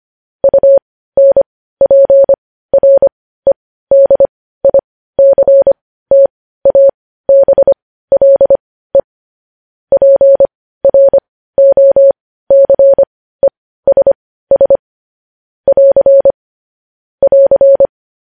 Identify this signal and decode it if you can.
Morse code, heard as a keyed tone, about 25 wpm, spaced as for 15 wpm, 550 Hz. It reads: UNPREDICTABLE PROCESS <AR> <AR>
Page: News Headlines in Morse Code at 15 WPM